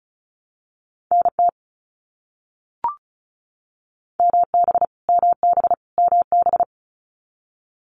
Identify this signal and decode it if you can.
Morse code, heard as a keyed tone, about 34 wpm, 700 Hz. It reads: NT MB MB MB
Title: Canadian Provinces and Territories - Repeated 3x - 35wpm